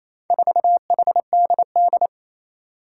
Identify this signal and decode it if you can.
Morse code, heard as a keyed tone, about 28 wpm, 700 Hz. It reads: 4HDD
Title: Four Challenging Letter-Number Combinations 28wpm